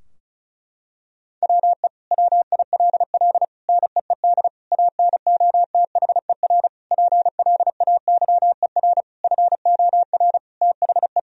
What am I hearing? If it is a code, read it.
Morse code, heard as a keyed tone, about 34 wpm, 700 Hz. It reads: WE WILL NEED ANOTHER PLAYER FOR THE
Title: Sentences from Top 1000 Words - Review Part 4 - 35wpm